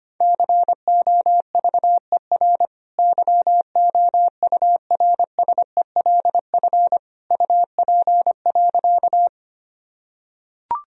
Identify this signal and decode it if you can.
Morse code, heard as a keyed tone, about 25 wpm, 700 Hz. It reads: COVER YOURSELF UP.
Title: Sentences from Top 1000 Words - Review Part 1 - 25wpm